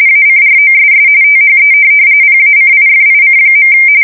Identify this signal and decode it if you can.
RTTY, about 45.45 baud. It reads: ACROSS THE PACIFIC.
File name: fsk170 / USS-New-Jersey